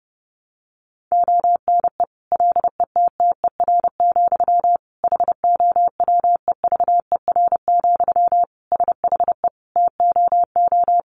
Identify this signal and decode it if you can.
Morse code, heard as a keyed tone, about 30 wpm, 700 Hz. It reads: ONE LETTER, HOWEVER, SHE TOO
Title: Sentences from Top 1200 Words 30wpm